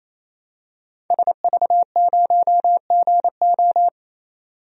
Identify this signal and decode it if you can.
Morse code, heard as a keyed tone, about 28 wpm, 700 Hz. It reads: SV0GO
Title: DXCC-100 Call Signs - 5 Characters - 28wpm